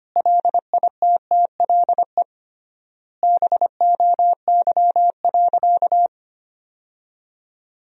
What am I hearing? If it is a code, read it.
Morse code, heard as a keyed tone, about 25 wpm, 700 Hz. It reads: LITTLE BOY.
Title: 3x Word-Spacing - Sentences from Top 200 Words - 25wpm